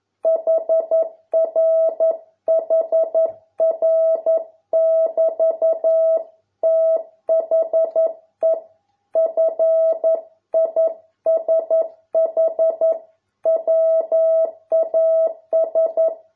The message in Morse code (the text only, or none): HR HR <BT> THE FISH WAS